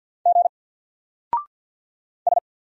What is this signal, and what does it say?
Morse code, heard as a keyed tone, about 50 wpm, 700 Hz. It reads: G S
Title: Single Letter-Number 50wpm